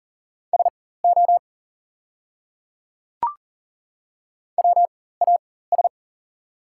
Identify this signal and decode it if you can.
Morse code, heard as a keyed tone, about 40 wpm, spaced as for 19 wpm, 690 Hz. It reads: SO WAS
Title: 4x Character Spacing - Top 100 Words - 40wpm